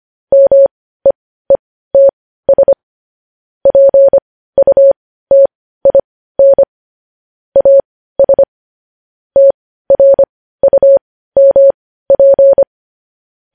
Morse code, heard as a keyed tone, about 25 wpm, spaced as for 15 wpm, 550 Hz. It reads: MEETS PUTIN AS TRUMP
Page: news-headlines-in-morse-code-at-15-wpm